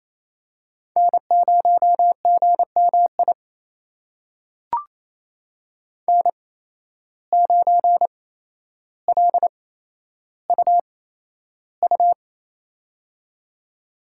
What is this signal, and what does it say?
Morse code, heard as a keyed tone, about 28 wpm, spaced as for 12 wpm, 700 Hz. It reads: N0GMI N9LUU
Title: USA Call Signs - 5 Characters - 8x Character Spacing - 28wpm